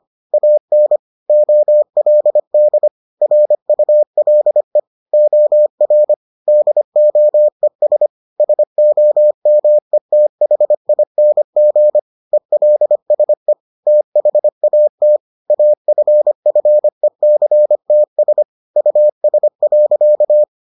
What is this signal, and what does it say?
Morse code, heard as a keyed tone, about 25 wpm, 600 Hz. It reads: AN OLD RULE OR DOES SOMETHING ELSE THAT AFFECTS US.